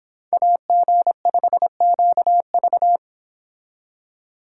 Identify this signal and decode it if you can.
Morse code, heard as a keyed tone, about 26 wpm, 700 Hz. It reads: AG5QV